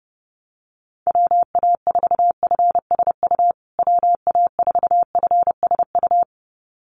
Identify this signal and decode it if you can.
Morse code, heard as a keyed tone, about 30 wpm, 700 Hz. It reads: WA4FSU WA4FSU
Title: USA Call Signs Repeated 2x 30wpm